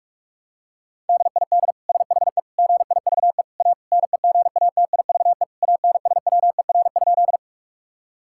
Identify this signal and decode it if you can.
Morse code, heard as a keyed tone, about 41 wpm, 700 Hz. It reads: DID SHE GIVE A NEGATIVE ANSWER?
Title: Sentences from Top 1000 Words - Review Part 6 - 45wpm